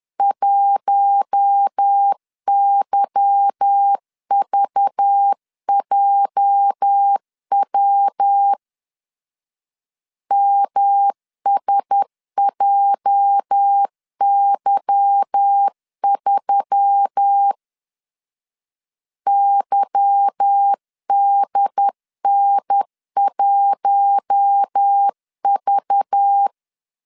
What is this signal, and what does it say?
Morse code, heard as a keyed tone, about 10 wpm, 800 Hz. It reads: 1YVJW MSJY3 YDN1V
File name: LEZIONI / 8 YJ45 / 8 YJ 10wpm 2651